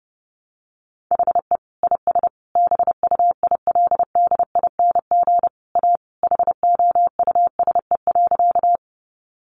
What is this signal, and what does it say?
Morse code, heard as a keyed tone, about 30 wpm, 700 Hz. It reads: HE IS BUILDING A HOUSE.
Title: Sentences from Top 700 Words 30wpm